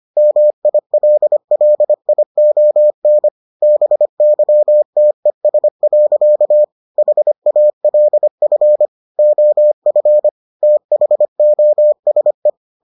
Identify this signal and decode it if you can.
Morse code, heard as a keyed tone, about 25 wpm, 600 Hz. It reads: MILLION BYTES. HALF OF THOSE